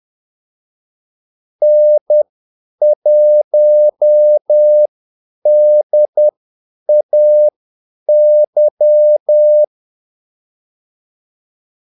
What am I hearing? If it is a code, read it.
Morse code, heard as a keyed tone, about 10 wpm, spaced as for 8 wpm, 600 Hz. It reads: N1DAY
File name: Callsigns-10-8wpm